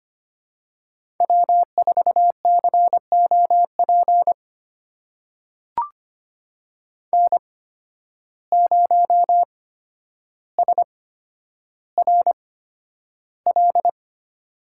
Morse code, heard as a keyed tone, about 25 wpm, spaced as for 11 wpm, 700 Hz. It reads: W4COP N0SRL
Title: USA Call Signs - 5 Characters - 8x Character Spacing - 25wpm